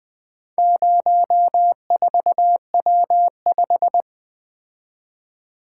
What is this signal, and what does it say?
Morse code, heard as a keyed tone, about 20 wpm, 700 Hz. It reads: 04W5
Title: Four Letter-Number Combinations - 20wpm